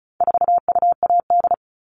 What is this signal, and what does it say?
Morse code, heard as a keyed tone, about 34 wpm, 700 Hz. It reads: 4UAD
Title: All Cut Numbers - 4-Digit 35wpm